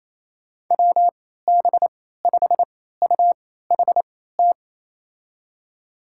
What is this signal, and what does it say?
Morse code, heard as a keyed tone, about 28 wpm, spaced as for 16 wpm, 700 Hz. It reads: WB5UHT
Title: USA Call Signs - 3x Character Spacing - 28wpm